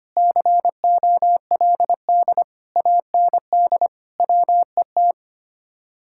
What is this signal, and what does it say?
Morse code, heard as a keyed tone, about 32 wpm, 700 Hz. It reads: COLD AND WET